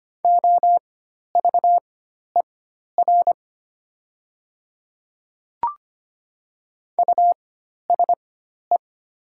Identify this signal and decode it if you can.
Morse code, heard as a keyed tone, about 25 wpm, spaced as for 12 wpm, 700 Hz. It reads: OVER USE
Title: 4x Character Spacing - Top 100 Words - 25wpm